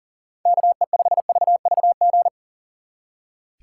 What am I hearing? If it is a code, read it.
Morse code, heard as a keyed tone, about 38 wpm, 700 Hz. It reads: KE5VVG
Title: Speed Racing - USA 2x3 Call Signs 40wpm